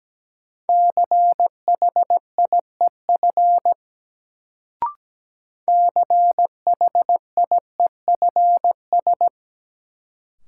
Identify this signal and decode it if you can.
Morse code, heard as a keyed tone, about 17 wpm, 700 Hz. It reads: CHIEF CHIEFS
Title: Sets of 5 Words Encore - New Word Drill - 17wpm